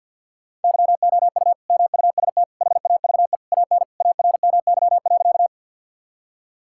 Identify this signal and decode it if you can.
Morse code, heard as a keyed tone, about 50 wpm, 700 Hz. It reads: YOU MUST HAVE AN ARMY.
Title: Sentences from Top 1000 Words - Review Part 5 - 50wpm